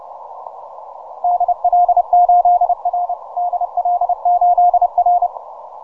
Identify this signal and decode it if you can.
Morse code, heard as a keyed tone, about 29 wpm, 710 Hz. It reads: DL8R DL8R
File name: simple-file-list / VP8SGI 12MCW